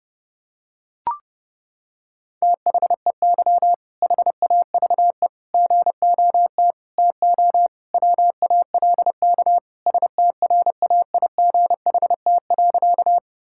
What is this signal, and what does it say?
Morse code, heard as a keyed tone, about 30 wpm, 700 Hz. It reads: THEY HAVE GOT TO WALK STRAIGHT.
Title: Sentences from Top 1000 Words - Review Part 6 - 30wpm